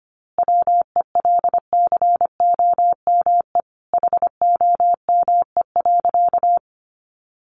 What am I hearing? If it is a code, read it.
Morse code, heard as a keyed tone, about 25 wpm, 700 Hz. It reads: WELCOME HOME.
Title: A Sound Of Thunder 25wpm